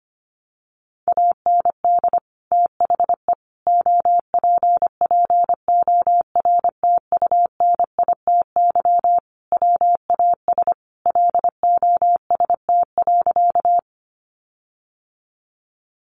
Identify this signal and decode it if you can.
Morse code, heard as a keyed tone, about 25 wpm, 700 Hz. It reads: AND THE OPPORTUNITY WAS LOST.
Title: Sentences from Top 1400 Words 25wpm